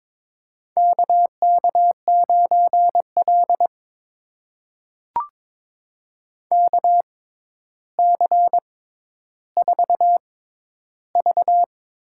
Morse code, heard as a keyed tone, about 23 wpm, spaced as for 11 wpm, 700 Hz. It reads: KK9L KC4V
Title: USA Call Signs - 6x Character Spacing - 22wpm